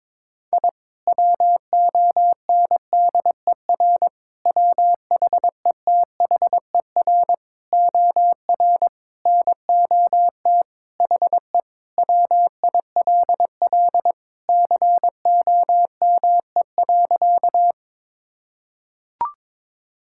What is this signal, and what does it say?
Morse code, heard as a keyed tone, about 22 wpm, 700 Hz. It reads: I WONDER WHETHER OR NOT HE WILL COME.
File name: practice / 22wpm / Sentences with 3-Word Phrases - 22wpm